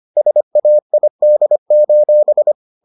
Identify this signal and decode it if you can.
Morse code, heard as a keyed tone, about 25 wpm, 600 Hz. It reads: SAID: